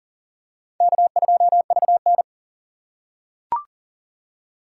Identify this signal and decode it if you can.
Morse code, heard as a keyed tone, about 37 wpm, 700 Hz. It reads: K2VN
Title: USA Call Signs - 4 Characters - 4x Character Spacing - 40wpm